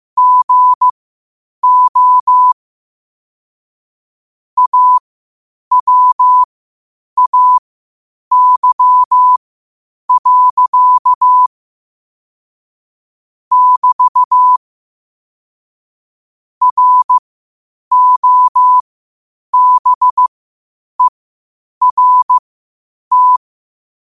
Morse code, heard as a keyed tone, about 15 wpm, spaced as for 8 wpm, 1000 Hz. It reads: GO AWAY. <BT> ROBERT
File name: qotdfiles / 8wpm